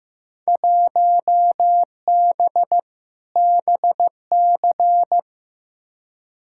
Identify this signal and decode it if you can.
Morse code, heard as a keyed tone, about 15 wpm, 700 Hz. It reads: 1B BC
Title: Field Day Exchanges - 15wpm